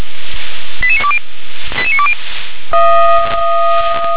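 Five-tone selective call (ZVEI), heard as quick runs of five tones, tones of about 70 ms. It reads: 88920 88920
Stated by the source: ZVEI five-tone call received via SDR.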